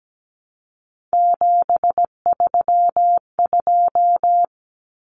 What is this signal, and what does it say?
Morse code, heard as a keyed tone, about 17 wpm, 700 Hz. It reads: 732
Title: Common Cut Numbers - 3-Digit 17wpm